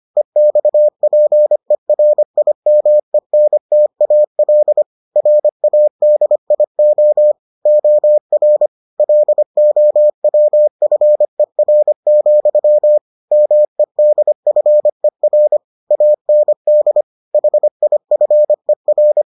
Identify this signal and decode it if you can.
Morse code, heard as a keyed tone, about 25 wpm, 600 Hz. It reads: EXPERIMENTAL RADIO OR LOWFER, MEDFER AND HIFER